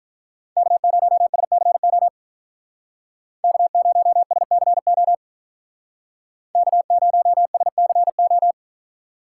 Morse code, heard as a keyed tone, about 44 wpm, 700 Hz. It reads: K0SCO K0SCO K0SCO